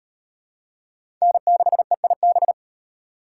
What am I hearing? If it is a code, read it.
Morse code, heard as a keyed tone, about 35 wpm, 700 Hz. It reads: N6EIB